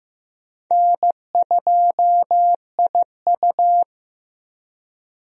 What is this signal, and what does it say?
Morse code, heard as a keyed tone, about 15 wpm, 700 Hz. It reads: N2IU
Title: USA Call Signs - 4 Characters - Repeated 2x - 15wpm